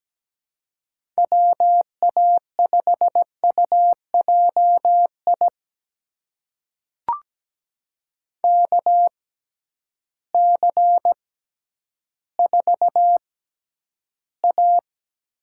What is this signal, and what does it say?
Morse code, heard as a keyed tone, about 17 wpm, spaced as for 10 wpm, 700 Hz. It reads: WA5UJI KC4A
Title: USA Call Signs - 6x Character Spacing - 17wpm